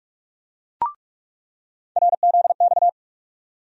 Morse code, heard as a keyed tone, about 46 wpm, 700 Hz. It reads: RZX
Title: USA Call Sign Postfixes - 3 Letters 45wpm